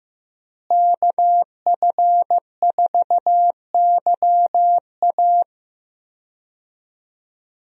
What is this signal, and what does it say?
Morse code, heard as a keyed tone, about 15 wpm, 700 Hz. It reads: KF4YA
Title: USA Call Signs - 5 Characters - Repeated 3x - 15wpm